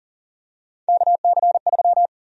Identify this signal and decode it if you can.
Morse code, heard as a keyed tone, about 40 wpm, 700 Hz. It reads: KC3
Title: USA Call Sign Prefixes - 40wpm